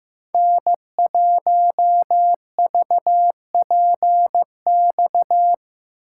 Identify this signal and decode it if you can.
Morse code, heard as a keyed tone, about 18 wpm, 700 Hz. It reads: N1VPX
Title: Speed Racing - USA 1x3 Call Signs 15wpm